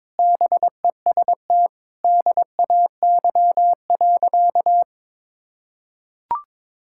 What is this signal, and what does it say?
Morse code, heard as a keyed tone, about 22 wpm, 700 Hz. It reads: BEST DAY.
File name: practice / 22wpm / Sentences with 3-Word Phrases - 22wpm